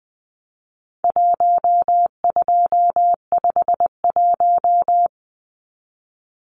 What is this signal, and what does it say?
Morse code, heard as a keyed tone, about 20 wpm, 700 Hz. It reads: 1251
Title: Numbers - Contest Style - 4-Digit 20wpm